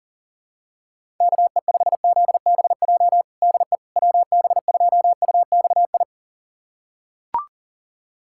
Morse code, heard as a keyed tone, about 37 wpm, 700 Hz. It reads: KE5ZBJ DE WB2UXI
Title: USA Callsign DE Callsign 40wpm